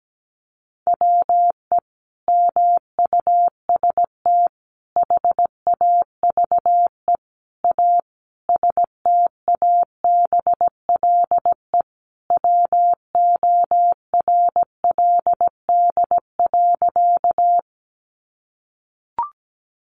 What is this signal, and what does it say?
Morse code, heard as a keyed tone, about 17 wpm, 700 Hz. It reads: WE MUST HAVE A STABLE WORLD.
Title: Sentences from Top 1500 Words 17wpm